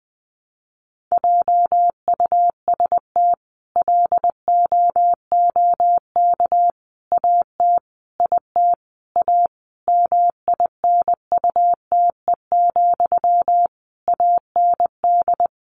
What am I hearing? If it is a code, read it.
Morse code, heard as a keyed tone, about 20 wpm, 700 Hz. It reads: JUST LOOK AT IT A MINUTE, AND
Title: Sentences from Top 1000 Words 20wpm